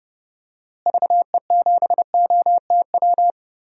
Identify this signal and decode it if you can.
Morse code, heard as a keyed tone, about 30 wpm, 700 Hz. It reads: VE7OTW
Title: DXCC-100 Call Signs - 6 Characters - Repeated 3x - 30wpm